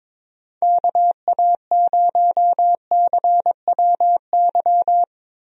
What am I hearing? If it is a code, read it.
Morse code, heard as a keyed tone, about 21 wpm, 700 Hz. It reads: KA0CWY
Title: USA Call Signs - 6 Characters - Repeated 3x - 22wpm